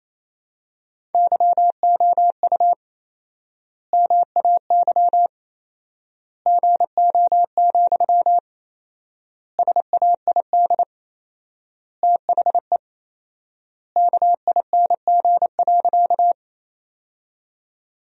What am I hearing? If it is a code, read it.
Morse code, heard as a keyed tone, about 28 wpm, 700 Hz. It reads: YOU MAY GO, SAID THE KING.
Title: 4x Word-Spacing - Sentences from Top 400 Words - 28wpm